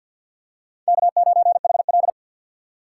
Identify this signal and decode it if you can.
Morse code, heard as a keyed tone, about 52 wpm, 700 Hz. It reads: K9HL
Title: USA Call Signs - 4 Characters - 6x Character Spacing - 50wpm